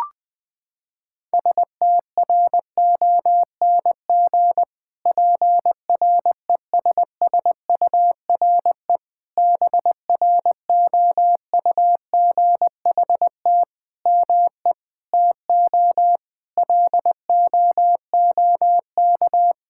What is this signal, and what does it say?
Morse code, heard as a keyed tone, about 20 wpm, 700 Hz. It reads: STRONG PRESSURE BROUGHT ME TO LOOK
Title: Sentences from Top 1000 Words - Review Part 6 - 20wpm